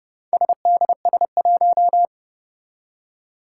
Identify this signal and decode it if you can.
Morse code, heard as a keyed tone, about 30 wpm, 700 Hz. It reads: SDS1
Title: Four Challenging Letter-Number Combinations 30wpm